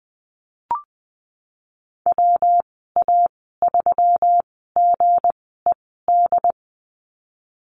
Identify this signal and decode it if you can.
Morse code, heard as a keyed tone, about 20 wpm, spaced as for 15 wpm, 700 Hz. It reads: WA3GED
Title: 186 Lesson 30g - Letter b - Call Signs 20wpm